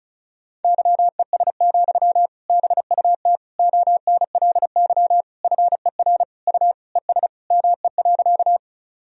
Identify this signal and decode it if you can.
Morse code, heard as a keyed tone, about 36 wpm, 700 Hz. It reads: YES, BUT ONLY FER U ES ME.
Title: Sentences from Top 500 Words and Abbreviations 35wpm